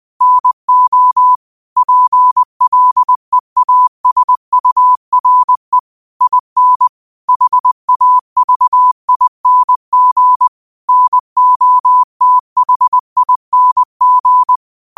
Morse code, heard as a keyed tone, about 20 wpm, 1000 Hz. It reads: NO PLEASURE IN HAVING NOTHING